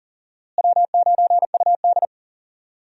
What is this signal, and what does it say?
Morse code, heard as a keyed tone, about 40 wpm, 700 Hz. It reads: W9UD
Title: USA Call Signs - 4 Characters - Repeated 3x - 40wpm